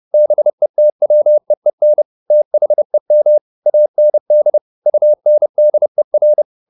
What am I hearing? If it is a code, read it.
Morse code, heard as a keyed tone, about 30 wpm, 600 Hz. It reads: BETWEEN THEM AND UNDER
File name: the old man and the sea / the old man and the sea 1 30wpm